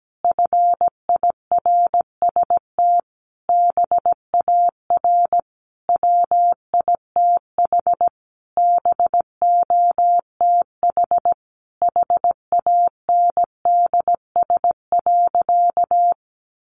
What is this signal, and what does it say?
Morse code, heard as a keyed tone, about 17 wpm, 700 Hz. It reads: FIRST BAR WITH BOTH HANDS.
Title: Easy Phrases 17wpm